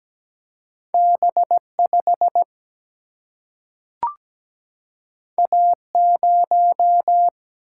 Morse code, heard as a keyed tone, about 17 wpm, 700 Hz. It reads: B5 A0
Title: Single-Letter Single-Number 17wpm